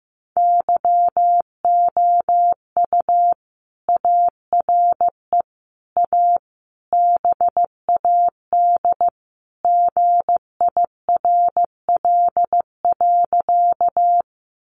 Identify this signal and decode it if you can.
Morse code, heard as a keyed tone, about 15 wpm, 700 Hz. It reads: YOU ARE A BAD GIRL.
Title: Sentences from Top 700 Words 15wpm